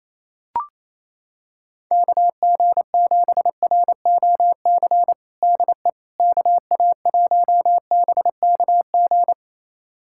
Morse code, heard as a keyed tone, about 27 wpm, 700 Hz. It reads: KG7ROC DE KA1BKG
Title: USA Callsign DE Callsign - 6 Characters - 28wpm